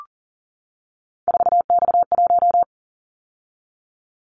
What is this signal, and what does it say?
Morse code, heard as a keyed tone, about 38 wpm, 700 Hz. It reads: 4X1
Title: Callsign Prefixes for Top 50 DXCC - 40wpm